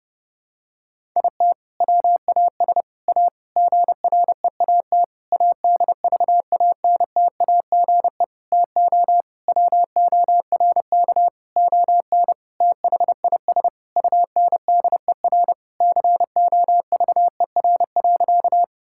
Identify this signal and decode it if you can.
Morse code, heard as a keyed tone, about 30 wpm, 700 Hz. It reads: IT WAS A GREAT ADVANTAGE TO WORK ON THIS UNDER COVER.
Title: Sentences from Top 1000 Words - Review Part 6 - 30wpm